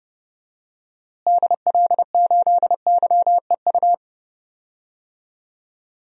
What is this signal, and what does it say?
Morse code, heard as a keyed tone, about 30 wpm, 700 Hz. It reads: DL8YEU
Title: DXCC-100 Call Signs - 6 Characters - Repeated 2x - 30wpm